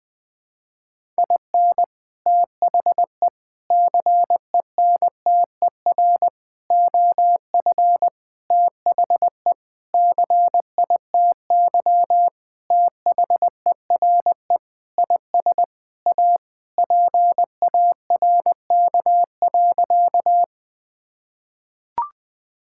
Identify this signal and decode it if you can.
Morse code, heard as a keyed tone, about 20 wpm, 700 Hz. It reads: IN THE CENTER OF THE CITY THERE IS A PARK.
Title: Sentences with 5-Word Phrases - 20wpm